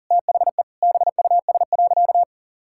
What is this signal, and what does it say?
Morse code, heard as a keyed tone, about 40 wpm, 700 Hz. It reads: THE BUS.
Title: Sentences from Top 1000 Words - Review Part 6 - 40wpm